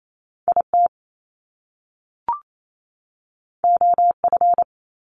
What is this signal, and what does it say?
Morse code, heard as a keyed tone, about 28 wpm, 700 Hz. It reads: I T O F
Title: Top Two Letter Combinations in Words 28wpm